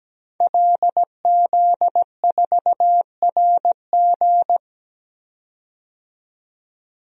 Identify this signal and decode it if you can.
Morse code, heard as a keyed tone, about 17 wpm, 700 Hz. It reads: LZ4RG
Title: DXCC-100 Call Signs - 5 Characters - 17wpm